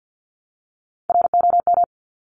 Morse code, heard as a keyed tone, about 50 wpm, 700 Hz. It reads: ROM
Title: Top Three Letter Combinations in Words 50wpm